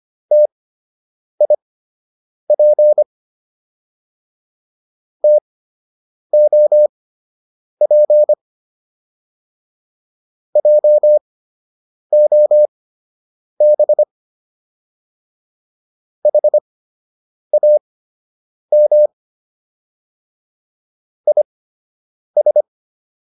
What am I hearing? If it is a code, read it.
Morse code, heard as a keyed tone, about 25 wpm, spaced as for 8 wpm, 600 Hz. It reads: TIP TOP JOB HAM IS